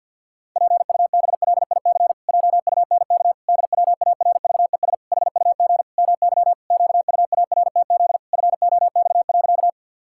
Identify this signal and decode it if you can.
Morse code, heard as a keyed tone, about 52 wpm, 700 Hz. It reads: PUBLIC JUNK DWARVES HUG MY QUARTZ FOX.
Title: Pangram Challenge v2 - 50wpm